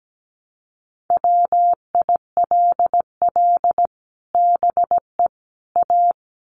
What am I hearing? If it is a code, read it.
Morse code, heard as a keyed tone, about 17 wpm, 700 Hz. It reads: WILL BE A
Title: Sets of 3 Words 17wpm